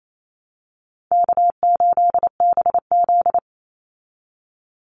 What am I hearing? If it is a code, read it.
Morse code, heard as a keyed tone, about 28 wpm, 700 Hz. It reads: K8BZ
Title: Easy Call Signs Repeated 3x - 28wpm